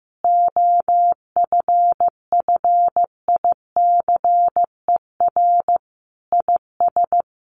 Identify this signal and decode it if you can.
Morse code, heard as a keyed tone, about 15 wpm, 700 Hz. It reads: OFFICER IS
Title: Sentences from Top 1400 Words 15wpm